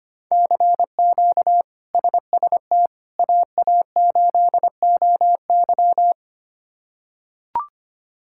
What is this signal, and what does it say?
Morse code, heard as a keyed tone, about 25 wpm, 700 Hz. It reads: CQ SST AA8OY
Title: K1USN SST QSO Practice - 25wpm